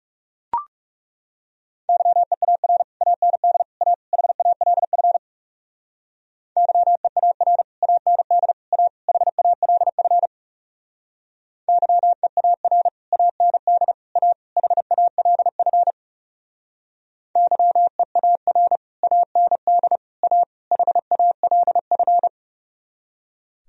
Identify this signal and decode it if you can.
Morse code, heard as a keyed tone, about 38 wpm, 700 Hz. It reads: YEAR AND A HALF YEAR AND A HALF YEAR AND A HALF YEAR AND A HALF